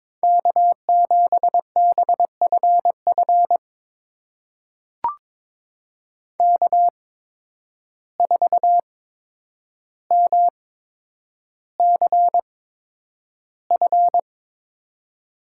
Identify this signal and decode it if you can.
Morse code, heard as a keyed tone, about 22 wpm, spaced as for 9 wpm, 700 Hz. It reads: K7BFF K4MCF
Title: USA Call Signs - 5 Characters - 8x Character Spacing - 22wpm